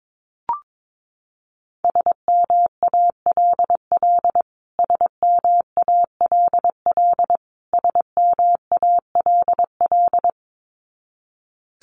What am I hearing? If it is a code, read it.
Morse code, heard as a keyed tone, about 22 wpm, 700 Hz. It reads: SMALL SMALL SMALL
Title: Top 500 Words Repeated 22wpm